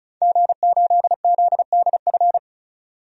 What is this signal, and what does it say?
Morse code, heard as a keyed tone, about 34 wpm, 700 Hz. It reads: G8ZDF
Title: DXCC-100 Call Signs - 5 Characters - 35wpm